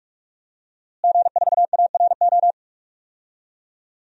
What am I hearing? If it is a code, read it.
Morse code, heard as a keyed tone, about 46 wpm, 700 Hz. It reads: G4ARO
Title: DXCC-100 Call Signs - 5 Characters - Repeated 2x - 45wpm